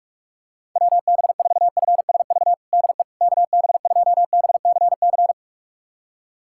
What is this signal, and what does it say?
Morse code, heard as a keyed tone, about 46 wpm, 700 Hz. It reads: WB4FSV DE KB2BCC
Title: USA Callsign DE Callsign - 6 Characters - 45wpm